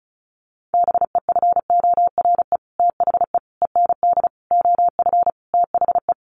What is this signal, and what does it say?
Morse code, heard as a keyed tone, about 34 wpm, 700 Hz. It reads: BEFORE THE END OF THE
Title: Sets of 5 Words Encore 35wpm